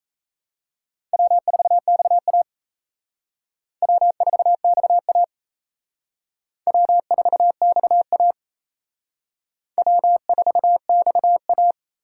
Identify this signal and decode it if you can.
Morse code, heard as a keyed tone, about 34 wpm, 700 Hz. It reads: W4XA W4XA W4XA W4XA
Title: Speed Racing - USA 1x2 Call Signs 28wpm